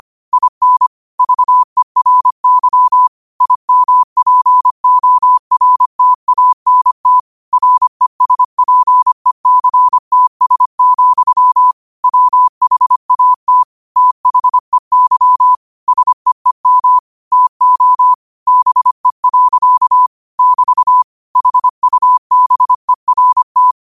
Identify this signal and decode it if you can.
Morse code, heard as a keyed tone, about 25 wpm, 1000 Hz. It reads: IN VERY IMPORTANT RESPECTS, WHAT THEY SEEM TO BE. <BT> HUBERT
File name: qotdfiles / 25wpm